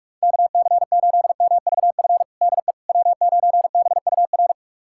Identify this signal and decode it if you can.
Morse code, heard as a keyed tone, about 46 wpm, 700 Hz. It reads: KC8MVF DE W9BUR
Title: USA Callsign DE Callsign 45wpm